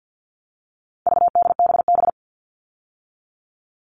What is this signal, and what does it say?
Morse code, heard as a keyed tone, about 48 wpm, 700 Hz. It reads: VDBB
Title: All Cut Numbers - 4-Digit 50wpm